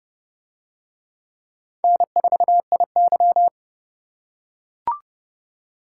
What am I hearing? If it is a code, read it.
Morse code, heard as a keyed tone, about 29 wpm, 700 Hz. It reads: N4IY